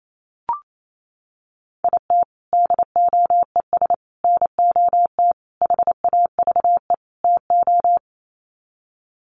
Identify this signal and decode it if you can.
Morse code, heard as a keyed tone, about 28 wpm, 700 Hz. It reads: IT DOES NOT HAVE TO
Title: Sets of 5 Words 28wpm